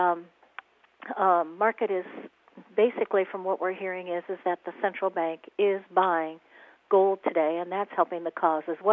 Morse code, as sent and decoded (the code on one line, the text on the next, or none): none